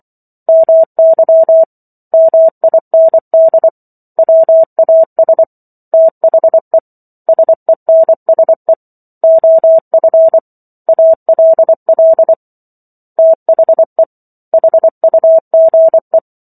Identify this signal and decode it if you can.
Morse code, heard as a keyed tone, about 24 wpm, 650 Hz. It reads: MY MIND WAS THE SENSE OF ALL THE HUGE
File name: Chapter 15 - 24 WPM